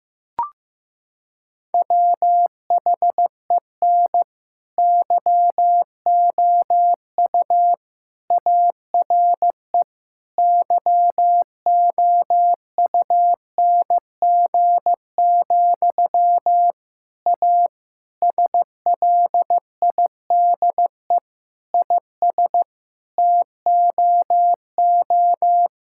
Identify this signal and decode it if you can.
Morse code, heard as a keyed tone, about 15 wpm, 700 Hz. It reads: WHEN YOU ARE YOUNG, A SLIDE IS TOO
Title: Sentences from Top 1600 Words 15wpm